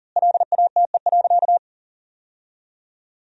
Morse code, heard as a keyed tone, about 40 wpm, 700 Hz. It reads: LATE.
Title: 3x Word-Spacing - Sentences from Top 300 Words - 40wpm